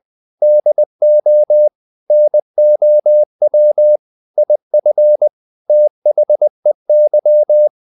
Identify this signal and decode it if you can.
Morse code, heard as a keyed tone, about 20 wpm, 600 Hz. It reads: DO NOW IF THEY
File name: the old man and the sea part 2 20wpm